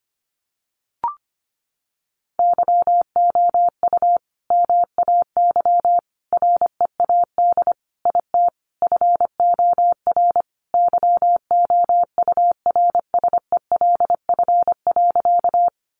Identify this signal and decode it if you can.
Morse code, heard as a keyed tone, about 25 wpm, 700 Hz. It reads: YOU MAY READ IT FOR YOURSELF.
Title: Sentences from Top 600 Words 25wpm